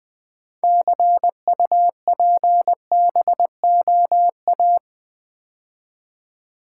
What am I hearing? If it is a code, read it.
Morse code, heard as a keyed tone, about 20 wpm, 700 Hz. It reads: CUPBOA
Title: Build A Word - 2 Letters At A Time - 8-Letter Words - 20wpm